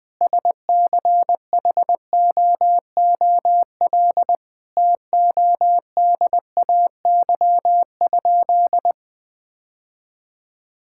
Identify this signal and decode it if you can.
Morse code, heard as a keyed tone, about 20 wpm, 700 Hz. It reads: SCHOOL TODAY?
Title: Sentences from Top 1000 Words - Review Part 5 - 20wpm